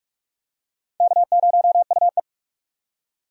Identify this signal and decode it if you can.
Morse code, heard as a keyed tone, about 46 wpm, 700 Hz. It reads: K0UE
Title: USA Call Signs - 4 Characters - 45wpm